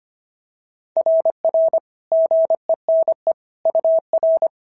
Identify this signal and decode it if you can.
Morse code, heard as a keyed tone, about 25 wpm, 650 Hz. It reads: RR GENE UR
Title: SOTA QSO Practice 25wpm